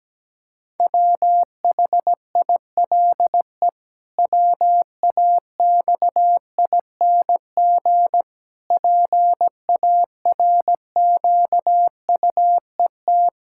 Morse code, heard as a keyed tone, about 17 wpm, 700 Hz. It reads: WHILE WAXING PARQUET
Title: Pangram Challenge v2 - 17wpm